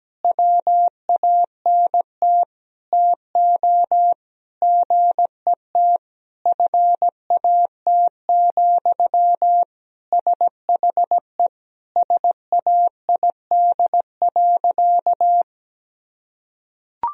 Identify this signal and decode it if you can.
Morse code, heard as a keyed tone, about 17 wpm, 700 Hz. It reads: WANT TO GET FAT, SHE SAID.
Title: Sentences from Top 1000 Words - Review Part 4 - 17wpm